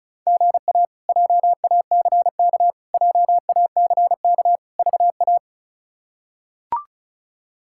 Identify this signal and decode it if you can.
Morse code, heard as a keyed tone, about 36 wpm, 700 Hz. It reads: GA JACK JACK VA
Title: K1USN SST QSO Practice - 35wpm